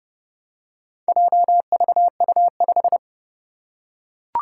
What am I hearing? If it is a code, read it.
Morse code, heard as a keyed tone, about 30 wpm, 700 Hz. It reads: JVU5
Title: Four Challenging Letter-Number Combinations 30wpm